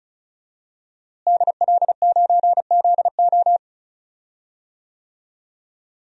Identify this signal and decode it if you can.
Morse code, heard as a keyed tone, about 34 wpm, 700 Hz. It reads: DL9ZO
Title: DXCC-100 Call Signs - 5 Characters - Repeated 2x - 35wpm